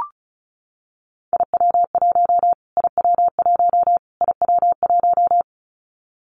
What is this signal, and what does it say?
Morse code, heard as a keyed tone, about 34 wpm, 700 Hz. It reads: IW1 IW1 IW1
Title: Callsign Prefixes for Top 25 DXCC - Repeated 3x - 35wpm